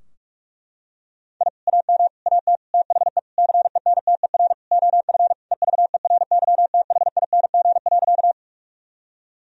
Morse code, heard as a keyed tone, about 46 wpm, 700 Hz. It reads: I AM AT THE CENTER OF EVERYTHING.
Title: Sentences from Top 1000 Words - Review Part 2 - 45wpm